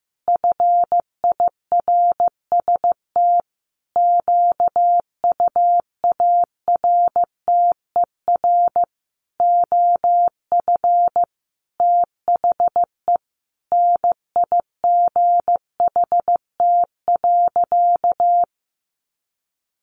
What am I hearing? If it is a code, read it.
Morse code, heard as a keyed tone, about 15 wpm, 700 Hz. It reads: FIRST QUARTER OF THE NIGHT.
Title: Sentences from Top 1300 Words 15wpm